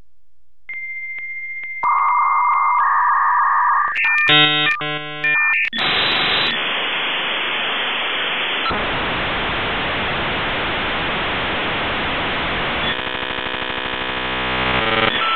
Dial-up modem handshake.
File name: 56k dialup